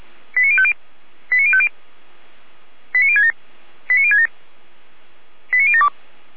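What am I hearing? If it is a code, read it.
Five-tone selective call (ZVEI), heard as quick runs of five tones, tones of about 70 ms. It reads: 89050 89050 89067 89067 89072
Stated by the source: ZVEI five-tone call received on SDR.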